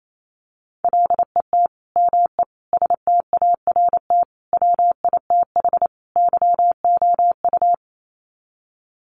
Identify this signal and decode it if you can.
Morse code, heard as a keyed tone, about 28 wpm, 700 Hz. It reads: LET ME START WITH YOU
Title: Sets of 5 Words Encore 28wpm